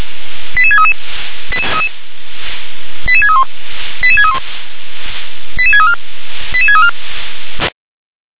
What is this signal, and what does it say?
Five-tone selective call (ZVEI), heard as quick runs of five tones, tones of about 70 ms. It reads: 88633 88631 88631 88634 88634